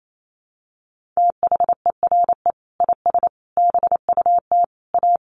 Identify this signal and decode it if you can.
Morse code, heard as a keyed tone, about 28 wpm, 700 Hz. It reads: THERE IS BUT A
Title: Sentences from Top 1400 Words 28wpm